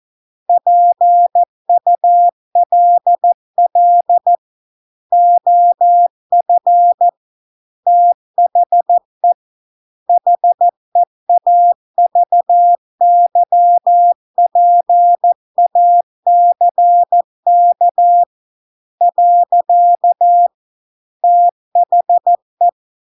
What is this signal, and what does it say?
Morse code, heard as a keyed tone, about 14 wpm, 700 Hz. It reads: PULL OF THE HEAVY PACK . THE